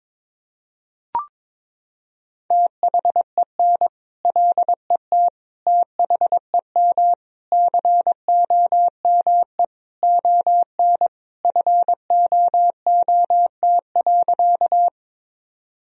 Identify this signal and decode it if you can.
Morse code, heard as a keyed tone, about 22 wpm, 700 Hz. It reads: THEN LET THEM COME ON FOOT.
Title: Sentences from Top 1000 Words - Review Part 3 - 22wpm